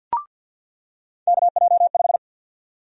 Morse code, heard as a keyed tone, about 47 wpm, 700 Hz. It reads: KJ5
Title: USA Call Sign Prefixes - 50wpm